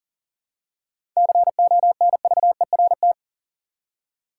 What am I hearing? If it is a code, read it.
Morse code, heard as a keyed tone, about 40 wpm, 700 Hz. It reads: CONVERT
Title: Seven Letter Words - 40wpm